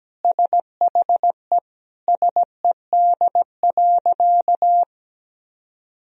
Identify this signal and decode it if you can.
Morse code, heard as a keyed tone, about 17 wpm, 700 Hz. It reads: SHE SED.
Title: Sentences from Top 500 Words and Abbreviations 17wpm